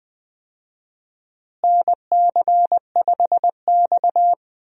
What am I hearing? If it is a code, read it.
Morse code, heard as a keyed tone, about 21 wpm, 700 Hz. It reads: NC5X
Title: USA Call Signs - 4 Characters - Repeated 3x - 20wpm